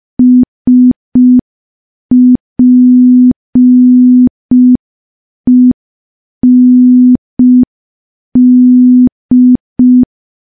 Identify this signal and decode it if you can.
Morse code, heard as a keyed tone, about 5 wpm, 250 Hz. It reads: SPEND
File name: morse / morse-training-05WPM